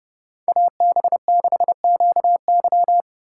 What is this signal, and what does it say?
Morse code, heard as a keyed tone, about 30 wpm, 700 Hz. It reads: AB6QY